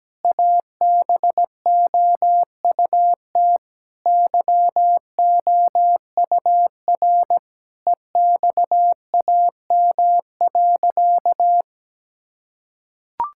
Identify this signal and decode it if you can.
Morse code, heard as a keyed tone, about 17 wpm, 700 Hz. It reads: ABOUT YOUR EXAM.
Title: Sentences from Top 1000 Words - Review Part 5 - 17wpm